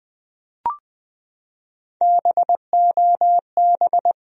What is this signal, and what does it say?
Morse code, heard as a keyed tone, about 20 wpm, 700 Hz. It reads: BOB
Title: USA Common Names - Max 4 Letters - 20wpm